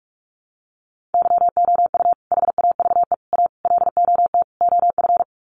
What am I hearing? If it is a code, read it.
Morse code, heard as a keyed tone, about 44 wpm, 700 Hz. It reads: YOU HAVE A LOT OF
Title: Sets of 5 Words Encore 45wpm